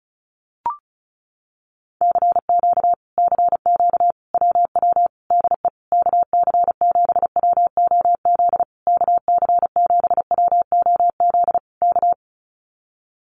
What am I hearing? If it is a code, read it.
Morse code, heard as a keyed tone, about 36 wpm, 700 Hz. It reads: CQ CQ WW DE KC7WOZ KC7WOZ K
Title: Common QSO Phrases Encore 35wpm